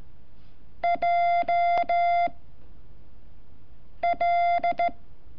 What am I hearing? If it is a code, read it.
Morse code, heard as a keyed tone, about 14 wpm, 690 Hz. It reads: J L
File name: curso / ejercicio 29